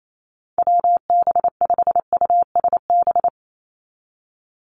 Morse code, heard as a keyed tone, about 28 wpm, 700 Hz. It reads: WB5USB
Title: NA SOTA Activator Callsigns 28wpm